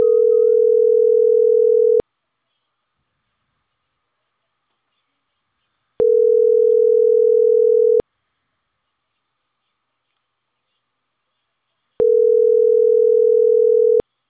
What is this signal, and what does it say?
Ringback tone (North America), 440 + 480 Hz.